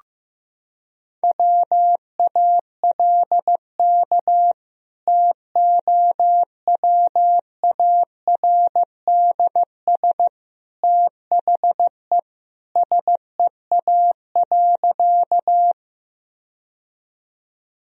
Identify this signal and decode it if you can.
Morse code, heard as a keyed tone, about 16 wpm, 700 Hz. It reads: WALK TOWARDS THE SEA.
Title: Sentences from Top 1000 Words - Review Part 2 - 15wpm